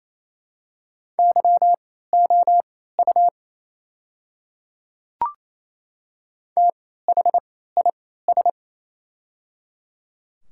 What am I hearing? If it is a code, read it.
Morse code, heard as a keyed tone, about 28 wpm, spaced as for 16 wpm, 700 Hz. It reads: YOU THIS
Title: 3x Character Spacing - Top 100 Words - 28wpm